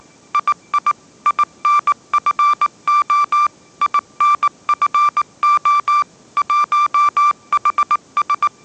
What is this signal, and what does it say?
Morse code, heard as a keyed tone, about 19 wpm, 1180 Hz. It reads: II INFO INFO 1HS
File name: mrx083 20wpm